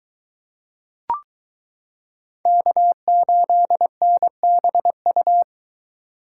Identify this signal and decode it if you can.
Morse code, heard as a keyed tone, about 23 wpm, 700 Hz. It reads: K8NBU